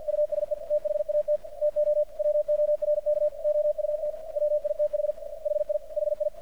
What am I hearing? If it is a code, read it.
Morse code, heard as a keyed tone, about 48 wpm, 600 Hz, with radio noise. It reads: USE THAT TO WORK OUT WITH HI HI
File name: archived / CRSnet